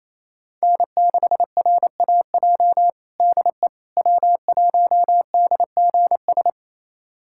Morse code, heard as a keyed tone, about 28 wpm, 700 Hz. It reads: N6RAJ DE W1DGS
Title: USA Callsign DE Callsign - 5 Characters - 28wpm